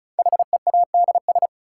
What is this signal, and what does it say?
Morse code, heard as a keyed tone, about 36 wpm, 700 Hz. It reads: HEADS
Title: Sentences from Top 1000 Words - Review Part 3 - New Word Drill - 35wpm